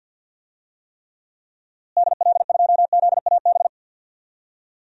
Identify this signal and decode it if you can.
Morse code, heard as a keyed tone, about 50 wpm, 700 Hz. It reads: DL2ZAB
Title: DXCC-100 Call Signs - 6 Characters - Repeated 3x - 50wpm